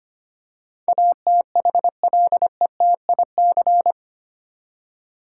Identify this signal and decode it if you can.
Morse code, heard as a keyed tone, about 25 wpm, 700 Hz. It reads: ATHLETIC
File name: Eight Letter Words - 25wpm